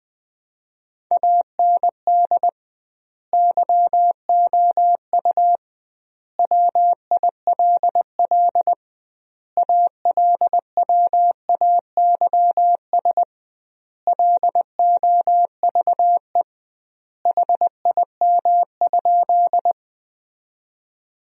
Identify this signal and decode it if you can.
Morse code, heard as a keyed tone, about 21 wpm, 700 Hz. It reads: AND YOU WILL ALWAYS LOVE HIM?
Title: 2x Word-Spacing - Sentences from Top 400 Words - 20wpm